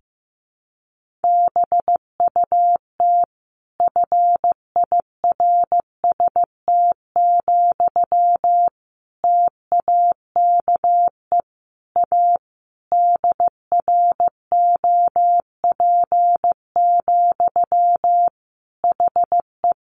Morse code, heard as a keyed tone, about 15 wpm, 700 Hz. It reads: BUT FIRST, TAKE A DROP, HE
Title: Sentences from Top 1400 Words 15wpm